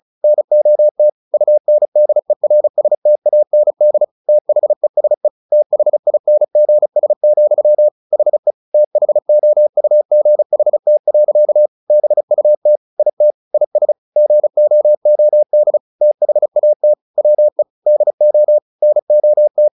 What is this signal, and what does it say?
Morse code, heard as a keyed tone, about 34 wpm, 600 Hz. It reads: NOT UNDERSTAND THESE THINGS, HE THOUGHT. BUT IT IS GOOD THAT WE DO NOT
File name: the old man and the sea 2 35wpm